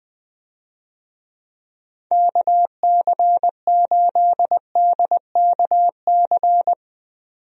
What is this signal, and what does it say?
Morse code, heard as a keyed tone, about 20 wpm, 700 Hz. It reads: KC8DKC
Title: USA Call Signs - 6 Characters - Repeated 3x - 20wpm